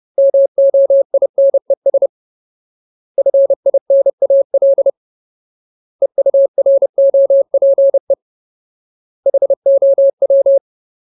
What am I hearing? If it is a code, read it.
Morse code, heard as a keyed tone, about 30 wpm, 550 Hz. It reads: MOINES FINAL EUROPE HOW
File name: Ultimate Word List / 30WPM Ultimate